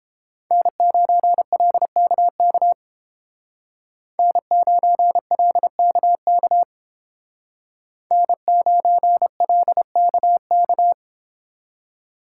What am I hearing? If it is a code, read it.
Morse code, heard as a keyed tone, about 30 wpm, 700 Hz. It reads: N9LKK N9LKK N9LKK